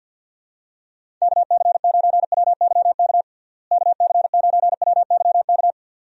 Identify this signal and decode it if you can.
Morse code, heard as a keyed tone, about 47 wpm, 700 Hz. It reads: KC9WYK KC9WYK
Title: USA Call Signs - 6 Characters - Repeated 2x - 50wpm